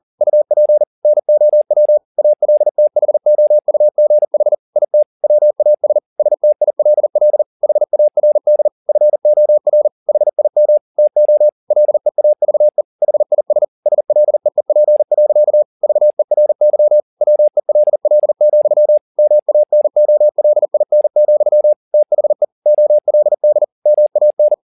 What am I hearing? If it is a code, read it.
Morse code, heard as a keyed tone, about 40 wpm, 600 Hz. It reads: UP NOW ALTHOUGH IT WAS STILL HARD FOR HIM TO LEAVE HIS SLEEP. VERY WELL, MANOLIN, THE OLD MAN